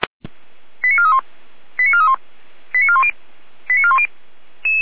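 Five-tone selective call (ZVEI), heard as quick runs of five tones, tones of about 70 ms. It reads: 89431 89431 89410 89410
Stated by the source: Five-tone ZVEI selective call, received with an SDR.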